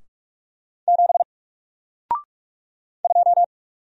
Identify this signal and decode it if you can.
Morse code, heard as a keyed tone, about 44 wpm, 700 Hz. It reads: 7 2
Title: Single Character (Letter:Number:Punctuation:Prosign) Alt 45wpm